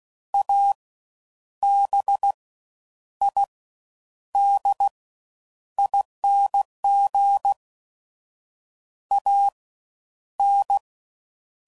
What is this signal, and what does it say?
Morse code, heard as a keyed tone, about 16 wpm, 800 Hz. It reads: A B I D ING A N
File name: Suffixes / Special INGs / Fast Ings